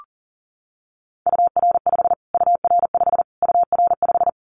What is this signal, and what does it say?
Morse code, heard as a keyed tone, about 37 wpm, 700 Hz. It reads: UR5 UR5 UR5
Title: Callsign Prefixes for Top 25 DXCC - Repeated 3x - 40wpm